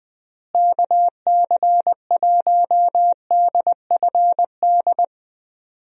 Morse code, heard as a keyed tone, about 21 wpm, 700 Hz. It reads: KC1DFD
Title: USA Call Signs - 6 Characters - Repeated 2x - 20wpm